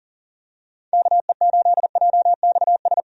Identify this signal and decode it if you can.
Morse code, heard as a keyed tone, about 40 wpm, 700 Hz. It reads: KE8JXS
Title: USA Call Signs - 6 Characters - 40wpm